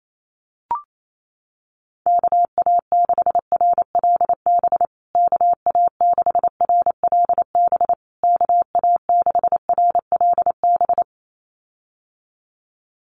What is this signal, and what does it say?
Morse code, heard as a keyed tone, about 28 wpm, 700 Hz. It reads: KA6RLB KA6RLB KA6RLB
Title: USA Call Signs Repeated 3x 28wpm